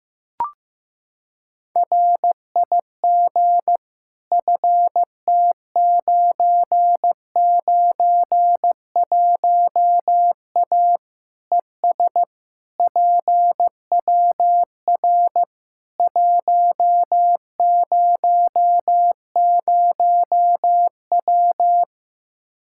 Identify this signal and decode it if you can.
Morse code, heard as a keyed tone, about 16 wpm, 700 Hz. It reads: RIG FT991A ES PWR 100W
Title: Practice QSO - Exchange 2 - Info - 15wpm